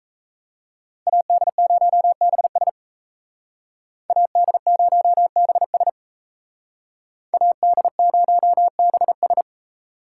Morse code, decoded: AD0BS AD0BS AD0BS

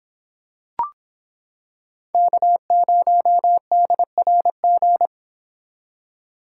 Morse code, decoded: K0DRG

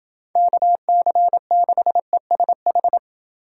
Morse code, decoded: KC6ESH

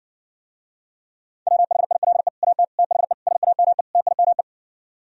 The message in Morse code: WHILE AT THE SAME TIME

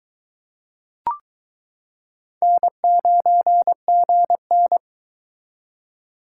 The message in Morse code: N9GN

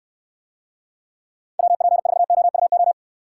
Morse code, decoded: WP4OUO

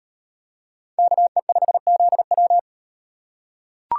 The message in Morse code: KE5ZW

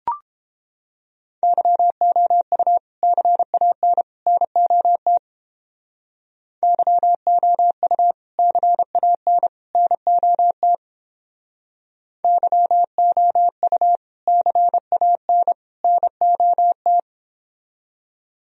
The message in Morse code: YOU CAN NOT YOU CAN NOT YOU CAN NOT

